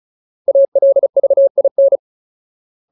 Morse code, heard as a keyed tone, about 34 wpm, 550 Hz. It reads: ALVIN
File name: Ultimate Word List / 35WPM Ultimate